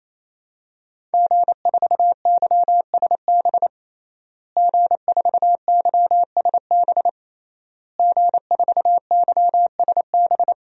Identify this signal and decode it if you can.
Morse code, heard as a keyed tone, about 27 wpm, 700 Hz. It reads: G4YSB G4YSB G4YSB
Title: DXCC-100 Call Signs - 5 Characters - Repeated 3x - 28wpm